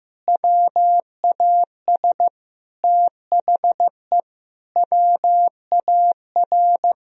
Morse code, heard as a keyed tone, about 16 wpm, 700 Hz. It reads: WAS THE WAR
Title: Sentences from Top 1000 Words - Review Part 3 - 15wpm